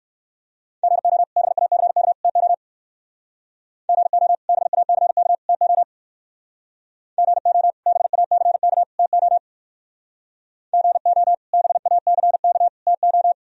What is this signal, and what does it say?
Morse code, heard as a keyed tone, about 61 wpm, 700 Hz. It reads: GO BACK TO GO BACK TO GO BACK TO GO BACK TO